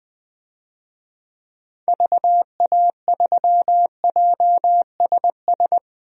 Morse code, heard as a keyed tone, about 20 wpm, 700 Hz. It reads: VA3JSS
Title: DXCC-100 Call Signs - 6 Characters - 20wpm